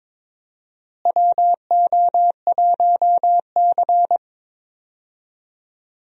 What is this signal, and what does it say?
Morse code, heard as a keyed tone, about 22 wpm, 700 Hz. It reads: WO1C
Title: USA Call Signs - 4 Characters - Repeated 3x - 22wpm